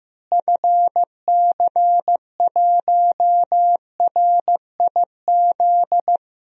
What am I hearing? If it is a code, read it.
Morse code, heard as a keyed tone, about 16 wpm, 700 Hz. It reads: FC1RIZ
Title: DXCC-100 Call Signs - 6 Characters - Repeated 2x - 15wpm